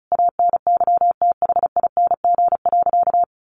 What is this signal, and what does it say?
Morse code, heard as a keyed tone, about 34 wpm, 700 Hz. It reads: ANYTHING.
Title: Sentences from Top 1500 Words 35wpm